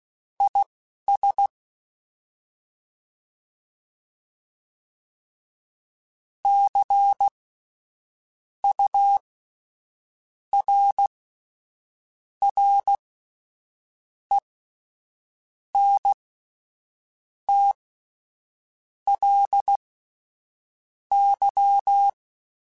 Morse code, heard as a keyed tone, about 16 wpm, spaced as for 6 wpm, 800 Hz. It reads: IS CURRENTLY
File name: Spaced Characters / Spaced Characters-mp3s / Weather 5 22 sp @10